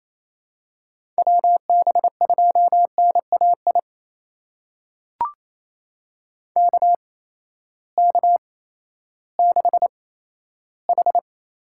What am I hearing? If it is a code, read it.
Morse code, heard as a keyed tone, about 28 wpm, spaced as for 14 wpm, 700 Hz. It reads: WB2NAI KK6H
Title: USA Call Signs - 6 Characters - 8x Character Spacing - 28wpm